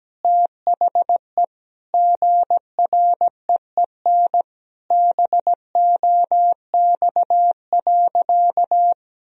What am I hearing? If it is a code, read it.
Morse code, heard as a keyed tone, about 17 wpm, 700 Hz. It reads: THE GREEN BOX.
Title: Sentences from Top 1000 Words - Review Part 2 - 17wpm